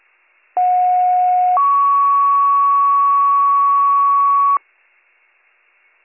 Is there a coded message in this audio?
Yes: two-tone page from 0.6 to 4.6 s, heard as a one-second tone, then a three-second tone, about 730 then 1120 Hz.